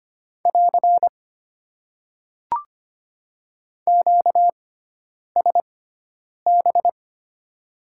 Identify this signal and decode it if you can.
Morse code, heard as a keyed tone, about 25 wpm, spaced as for 9 wpm, 700 Hz. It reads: <AR> QSB
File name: Top 90 QSO Elements - 6x Character Spacing - 25wpm